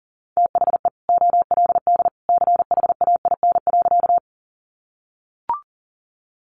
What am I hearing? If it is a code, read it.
Morse code, heard as a keyed tone, about 40 wpm, 700 Hz. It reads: THE OLD CHAIN.
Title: Sentences from Top 1500 Words 40wpm